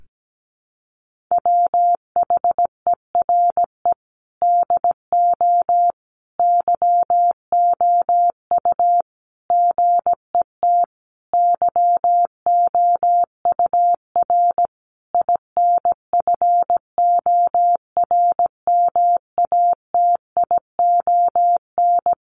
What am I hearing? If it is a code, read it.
Morse code, heard as a keyed tone, about 17 wpm, 700 Hz. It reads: WHERE DO YOU GET YOUR INFORMATION
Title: Sentences from Top 600 Words 17wpm